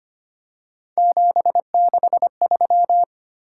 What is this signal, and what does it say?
Morse code, heard as a keyed tone, about 25 wpm, 700 Hz. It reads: 763
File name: Three Letter-Number Combinations - 25wpm